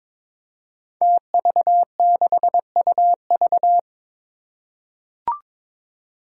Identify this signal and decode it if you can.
Morse code, heard as a keyed tone, about 23 wpm, 700 Hz. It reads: TV6UV
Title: Five Challenging Letter-Number Combinations 22wpm